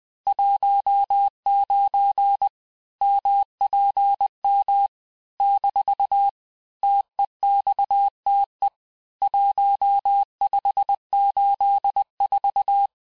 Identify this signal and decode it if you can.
Morse code, heard as a keyed tone, about 20 wpm, 800 Hz. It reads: 19 MPM - TEXTE 1584
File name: cours-du-lundi files / cours-1584-teste-chiffre-vitesse-19mpm